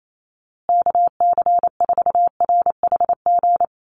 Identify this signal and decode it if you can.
Morse code, heard as a keyed tone, about 28 wpm, 700 Hz. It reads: KC4RHG
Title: USA Call Signs Repeated 2x 28wpm